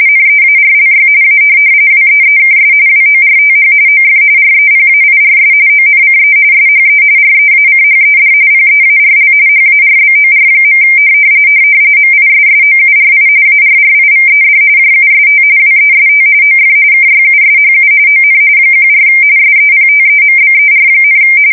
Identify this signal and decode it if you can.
RTTY, about 45.45 baud. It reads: NEARLY A WEEKS WORTH OF REPAIRS ONCE SHE REACHED PEARL HARBOR. MISSOURI ARRIVED JUST WEST OF KYUSHU ON 14 SEPTEMBER,
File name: fsk170 / USS-Missouri